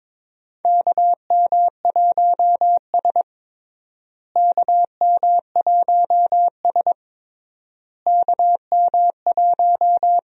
Morse code, heard as a keyed tone, about 22 wpm, 700 Hz. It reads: KM1S KM1S KM1
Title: USA Call Signs - 4 Characters - Repeated 3x - 22wpm